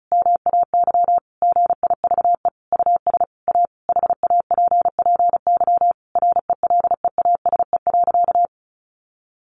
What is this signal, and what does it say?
Morse code, heard as a keyed tone, about 34 wpm, 700 Hz. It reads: MAY GIVE US A HAPPY RELEASE.
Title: Sentences from Top 1100 Words 35wpm